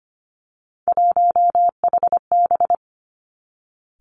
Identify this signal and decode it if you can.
Morse code, heard as a keyed tone, about 25 wpm, 700 Hz. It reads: 1HB
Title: Three Challenging Letter-Number Combinations 25wpm